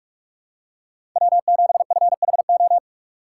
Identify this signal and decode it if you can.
Morse code, heard as a keyed tone, about 46 wpm, 700 Hz. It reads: W7FHO
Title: USA Call Signs - 5 Characters - 6x Character Spacing - 45wpm